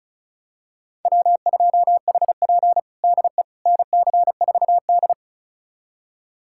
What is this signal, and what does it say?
Morse code, heard as a keyed tone, about 36 wpm, 700 Hz. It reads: W2HP DE NC4D
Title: USA Callsign DE Callsign - 4 Characters - 35wpm